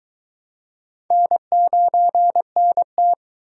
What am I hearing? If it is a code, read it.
Morse code, heard as a keyed tone, about 21 wpm, 700 Hz. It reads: N9NT